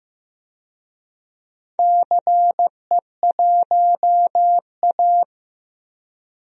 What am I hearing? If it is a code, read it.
Morse code, heard as a keyed tone, about 15 wpm, 700 Hz. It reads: CE1A
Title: DXCC-100 Call Signs - 4 Characters - Repeated 2x - 15wpm